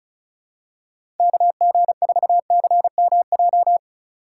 Morse code, heard as a keyed tone, about 34 wpm, 700 Hz. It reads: KG4CMJ
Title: USA Call Signs - 2x Character Spacing - 35wpm